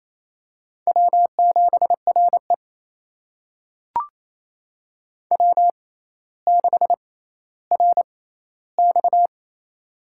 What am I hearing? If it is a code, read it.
Morse code, heard as a keyed tone, about 28 wpm, spaced as for 14 wpm, 700 Hz. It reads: W7RE W6RX
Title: USA Call Signs - 4 Characters - 6x Character Spacing - 28wpm